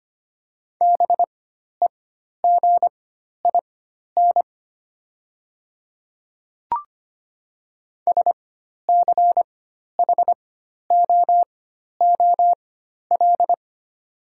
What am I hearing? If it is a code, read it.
Morse code, heard as a keyed tone, about 25 wpm, spaced as for 12 wpm, 700 Hz. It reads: BEGIN SCHOOL
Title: 4x Character Spacing - Top 300 Words - 25wpm